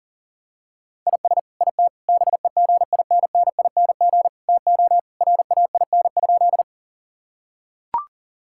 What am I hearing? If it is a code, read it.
Morse code, heard as a keyed tone, about 40 wpm, 700 Hz. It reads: IS IT BEGINNING TO RAIN?
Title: Sentences from Top 1000 Words - Review Part 5 - 40wpm